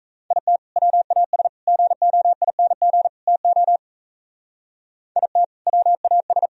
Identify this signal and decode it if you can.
Morse code, heard as a keyed tone, about 33 wpm, 700 Hz. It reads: IT WAS GOING TO IT WAS
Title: Speed-Racing - Sets of 4 Words 28wpm